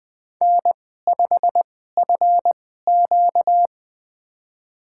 Morse code, heard as a keyed tone, about 20 wpm, spaced as for 15 wpm, 700 Hz. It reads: N5FQ
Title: USA Call Signs - 4 Characters - 2x Character Spacing - 20wpm